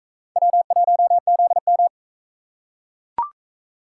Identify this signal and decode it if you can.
Morse code, heard as a keyed tone, about 42 wpm, 700 Hz. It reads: W1ZM